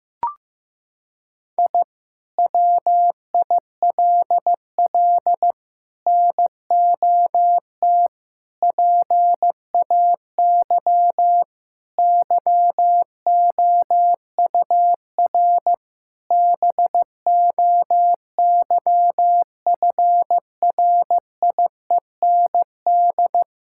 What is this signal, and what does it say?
Morse code, heard as a keyed tone, about 15 wpm, 700 Hz. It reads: I WILL NOT PAY YOUR BOYFRIEND
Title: Sentences from Top 1000 Words - Review Part 5 - 15wpm